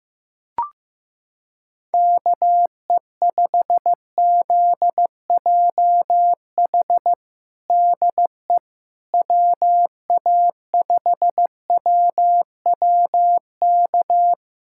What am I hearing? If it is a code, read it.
Morse code, heard as a keyed tone, about 15 wpm, 700 Hz. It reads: KE5ZJH DE WA5WWK
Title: USA Callsign DE Callsign - 6 Characters - 15wpm